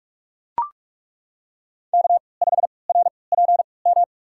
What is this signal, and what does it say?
Morse code, heard as a keyed tone, about 42 wpm, spaced as for 25 wpm, 700 Hz. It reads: K5RPM
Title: USA Call Signs - 3x Character Spacing - 45wpm